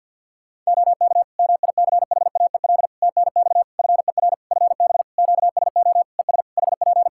Morse code, heard as a keyed tone, about 50 wpm, 700 Hz. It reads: OK MICHAEL TNX FER FB QSO ES HP
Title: Practice QSO - Exchange 4 - Ending - 50wpm